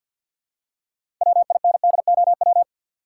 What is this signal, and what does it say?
Morse code, heard as a keyed tone, about 52 wpm, 700 Hz. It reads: WINDOW
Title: Six Letter Words - 50wpm